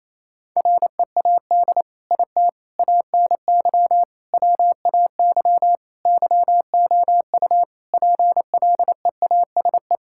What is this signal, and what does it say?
Morse code, heard as a keyed tone, about 28 wpm, 700 Hz. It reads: READ IT ANY WAY YOU PLEASE
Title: Sentences from Top 1600 Words 28wpm